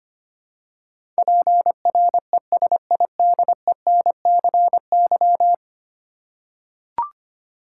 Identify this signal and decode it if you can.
Morse code, heard as a keyed tone, about 25 wpm, 700 Hz. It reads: PRESIDENCY
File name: 10 Letter Words - First 3 Letters Given - 25wpm